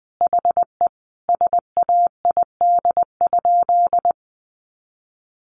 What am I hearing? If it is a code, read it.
Morse code, heard as a keyed tone, about 19 wpm, 700 Hz. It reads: HE SAID?
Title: Sentences from Top 700 Words 20wpm